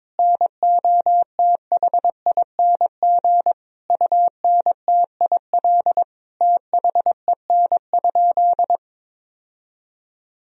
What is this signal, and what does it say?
Morse code, heard as a keyed tone, about 22 wpm, 700 Hz. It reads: NOTHING UNTIL THEN?
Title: Sentences from Top 1000 Words - Review Part 1 - 22wpm